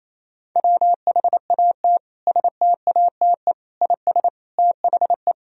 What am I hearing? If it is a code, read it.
Morse code, heard as a keyed tone, about 28 wpm, 700 Hz. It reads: WHAT STATE IS THE
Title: Sentences from Top 1000 Words - Review Part 2 - 28wpm